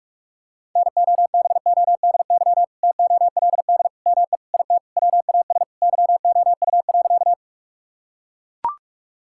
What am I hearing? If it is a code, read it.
Morse code, heard as a keyed tone, about 42 wpm, 700 Hz. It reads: NOBODY TOLD ME IT WAS YOU.
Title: Sentences from Top 1600 Words 45wpm